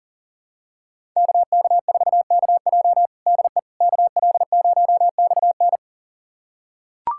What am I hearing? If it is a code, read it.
Morse code, heard as a keyed tone, about 40 wpm, 700 Hz. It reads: KK4KJ DE KL0XN
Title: USA Callsign DE Callsign - 5 Characters - 40wpm